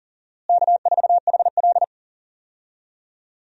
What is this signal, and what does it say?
Morse code, heard as a keyed tone, about 40 wpm, 700 Hz. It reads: K4HL
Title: USA Call Signs - 4 Characters - Repeated 3x - 40wpm